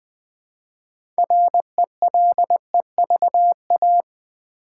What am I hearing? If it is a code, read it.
Morse code, heard as a keyed tone, about 20 wpm, 700 Hz. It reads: RELEVA
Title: Build A Word - 2 Letters At A Time - 8-Letter Words - 20wpm